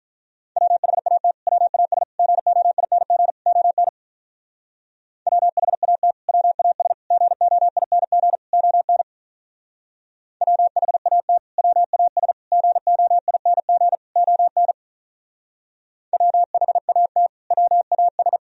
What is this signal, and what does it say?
Morse code, heard as a keyed tone, about 44 wpm, 700 Hz. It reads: WHAT WAS GOING ON WHAT WAS GOING ON WHAT WAS GOING ON WHAT WAS
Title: Speed-Racing - Sets of 4 Words 35wpm